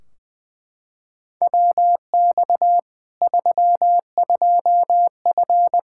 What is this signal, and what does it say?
Morse code, heard as a keyed tone, about 20 wpm, 700 Hz. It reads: WX 32F
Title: Practice QSO - Exchange 2 - Info - 20wpm